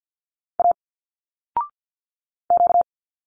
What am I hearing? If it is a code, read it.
Morse code, heard as a keyed tone, about 50 wpm, 700 Hz. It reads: A Q
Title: Single Letters 50wpm